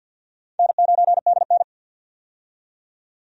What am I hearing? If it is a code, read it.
Morse code, heard as a keyed tone, about 51 wpm, 700 Hz. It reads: N9DN
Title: USA Call Signs - 4 Characters - 50wpm